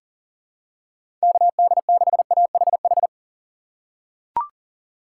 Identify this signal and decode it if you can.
Morse code, heard as a keyed tone, about 40 wpm, 700 Hz. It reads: KD6AHH
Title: USA Call Signs - 2x Character Spacing - 40wpm